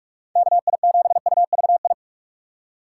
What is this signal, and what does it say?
Morse code, heard as a keyed tone, about 44 wpm, 700 Hz. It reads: KI7UVI